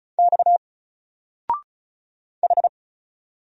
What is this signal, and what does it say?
Morse code, heard as a keyed tone, about 36 wpm, 700 Hz. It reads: X H
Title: Single Letter-Number Alt 35wpm